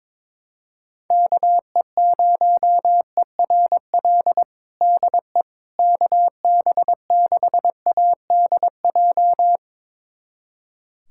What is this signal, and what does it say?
Morse code, heard as a keyed tone, about 21 wpm, 700 Hz. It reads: KE0ERL DE KB6ADJ
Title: USA Callsign DE Callsign 22wpm